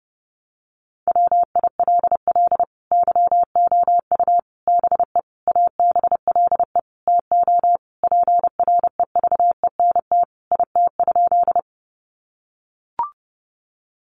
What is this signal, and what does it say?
Morse code, heard as a keyed tone, about 30 wpm, 700 Hz. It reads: WILL YOU BE ABLE TO PREVENT IT?
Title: Sentences from Top 1100 Words 30wpm